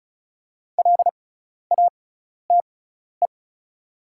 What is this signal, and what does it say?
Morse code, heard as a keyed tone, about 36 wpm, spaced as for 12 wpm, 700 Hz. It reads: LATE
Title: 6x Character Spacing - Top 500 Words - 35wpm